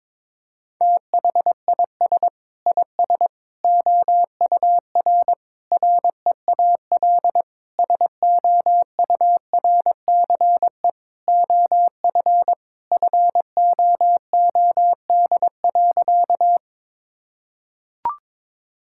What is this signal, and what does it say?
Morse code, heard as a keyed tone, about 22 wpm, 700 Hz. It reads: THIS IS OUR REAL SOURCE OF FOOD.
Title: Sentences from Top 1000 Words - Review Part 5 - 22wpm